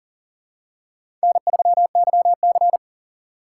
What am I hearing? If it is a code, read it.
Morse code, heard as a keyed tone, about 40 wpm, 700 Hz. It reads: N3YC
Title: USA Call Signs - 4 Characters - 3x Character Spacing - 40wpm